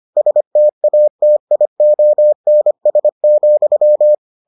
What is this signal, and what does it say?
Morse code, heard as a keyed tone, about 25 wpm, 600 Hz. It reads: STATIONS,